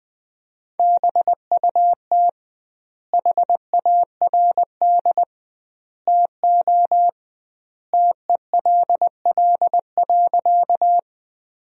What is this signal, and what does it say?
Morse code, heard as a keyed tone, about 20 wpm, 700 Hz. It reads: BUT HARD TO TELL.
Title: 2x Word-Spacing - Sentences from Top 300 Words - 20wpm